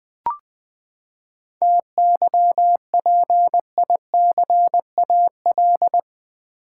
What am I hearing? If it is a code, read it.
Morse code, heard as a keyed tone, about 20 wpm, 700 Hz. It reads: TYPICAL
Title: Sentences from Top 1100 Words - New Word Drill Alt v2 - 20wpm